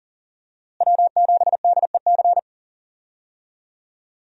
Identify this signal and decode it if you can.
Morse code, heard as a keyed tone, about 40 wpm, 700 Hz. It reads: W7DEC
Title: USA Call Signs - 5 Characters - Repeated 2x - 40wpm